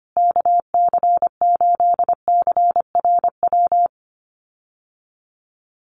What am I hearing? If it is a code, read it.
Morse code, heard as a keyed tone, about 25 wpm, 700 Hz. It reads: KC8CRW
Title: USA Call Signs Repeated 2x 25wpm